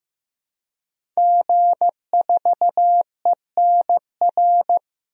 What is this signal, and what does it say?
Morse code, heard as a keyed tone, about 15 wpm, 700 Hz. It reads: G4ENR
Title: DXCC-100 Call Signs - 5 Characters - Repeated 3x - 15wpm